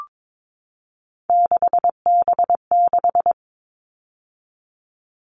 Morse code, heard as a keyed tone, about 22 wpm, 700 Hz. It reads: 6B6
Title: All Cut Numbers - 3-Digit 22wpm